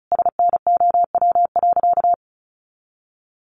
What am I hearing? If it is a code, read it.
Morse code, heard as a keyed tone, about 34 wpm, 700 Hz. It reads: SNOW.
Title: Easy Phrases 35wpm